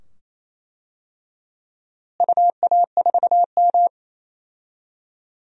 Morse code, heard as a keyed tone, about 27 wpm, 700 Hz. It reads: UA4M
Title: DXCC-100 Call Signs - 4 Characters - Repeated 2x - 28wpm